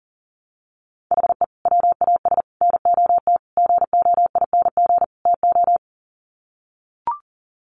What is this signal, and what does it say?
Morse code, heard as a keyed tone, about 40 wpm, 700 Hz. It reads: HE WAS NOT GOING TO
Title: Sets of 5 Words 40wpm